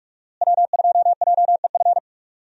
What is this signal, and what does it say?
Morse code, heard as a keyed tone, about 46 wpm, 700 Hz. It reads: W2JEF
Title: USA Call Signs - 5 Characters - Repeated 2x - 45wpm